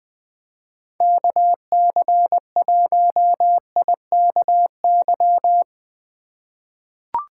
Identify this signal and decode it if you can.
Morse code, heard as a keyed tone, about 20 wpm, 700 Hz. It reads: KC1IKY
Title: USA Call Signs - 6 Characters - 3x Character Spacing - 20wpm